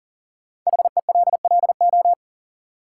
Morse code, heard as a keyed tone, about 40 wpm, 700 Hz. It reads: HELLO
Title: Sentences from Top 1100 Words - New Word Drill Alt - 40wpm